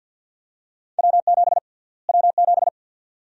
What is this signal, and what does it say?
Morse code, heard as a keyed tone, about 52 wpm, 700 Hz. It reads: W7 W7
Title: USA Call Sign Prefixes - Repeated 2x - 50wpm